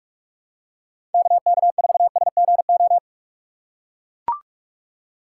Morse code, heard as a keyed tone, about 42 wpm, 700 Hz. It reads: KK4SGO